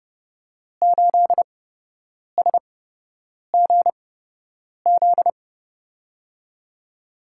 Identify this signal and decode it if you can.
Morse code, heard as a keyed tone, about 30 wpm, spaced as for 8 wpm, 700 Hz. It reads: 8SGZ
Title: USA Call Signs - 5 Characters - 8x Character Spacing - 30wpm